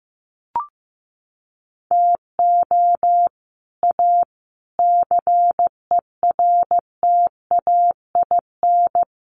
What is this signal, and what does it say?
Morse code, heard as a keyed tone, about 15 wpm, 700 Hz. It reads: TO A CERTAIN
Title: Sentences from Top 1500 Words 15wpm